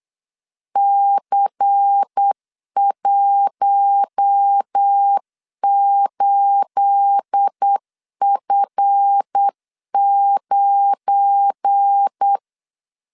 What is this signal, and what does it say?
Morse code, heard as a keyed tone, about 8 wpm, 800 Hz. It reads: C18F9
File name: CW / DET / 1 9 1500C S A 8wpm 5637